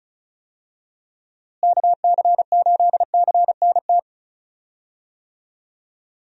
Morse code, heard as a keyed tone, about 36 wpm, 700 Hz. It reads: KC8CNT
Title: USA Call Signs - 6 Characters - Repeated 2x - 35wpm